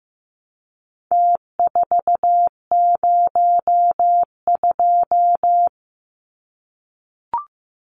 Morse code, heard as a keyed tone, about 15 wpm, 700 Hz. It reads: T402
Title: Numbers - Contest Style - 4-Digit 15wpm